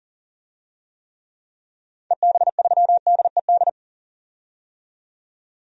Morse code, heard as a keyed tone, about 40 wpm, 700 Hz. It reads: EB3DED
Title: DXCC-100 Call Signs - 6 Characters - 40wpm